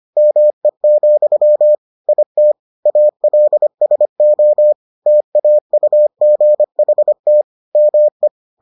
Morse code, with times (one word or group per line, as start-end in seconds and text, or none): 0.2-1.8 ME,
2.1-2.5 IT
2.9-4.7 ALSO
5.1-7.4 TAUGHT
7.8-8.3 ME